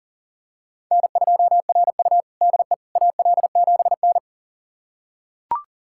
Morse code, heard as a keyed tone, about 40 wpm, 700 Hz. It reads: N2RU DE AL7N
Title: USA Callsign DE Callsign - 4 Characters - 40wpm